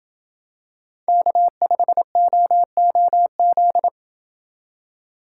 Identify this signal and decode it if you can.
Morse code, heard as a keyed tone, about 26 wpm, 700 Hz. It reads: K5OOZ